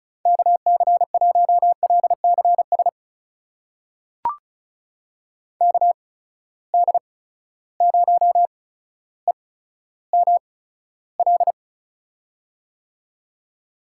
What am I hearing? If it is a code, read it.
Morse code, heard as a keyed tone, about 36 wpm, spaced as for 15 wpm, 700 Hz. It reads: KC1LCS KD0EML
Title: USA Call Signs - 6 Characters - 8x Character Spacing - 35wpm